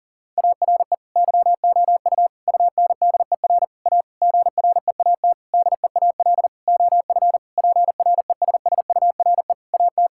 Morse code, heard as a keyed tone, about 37 wpm, 690 Hz. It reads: ARE YOU UNDER A GREAT DEAL OF PRESSURE AT
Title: Sentences from Top 1000 Words - Review Part 6 - 40wpm